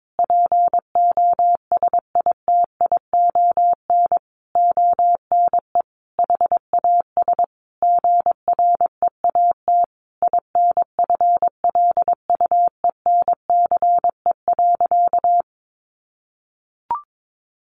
Morse code, heard as a keyed tone, about 22 wpm, 700 Hz. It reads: POSITION ONE HAS GREAT INFLUENCE.
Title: Sentences from Top 1100 Words 22wpm